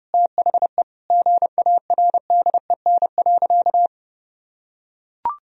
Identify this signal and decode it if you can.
Morse code, heard as a keyed tone, about 30 wpm, 700 Hz. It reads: THE GARDEN.
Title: Sentences from Top 1000 Words - Review Part 4 - 30wpm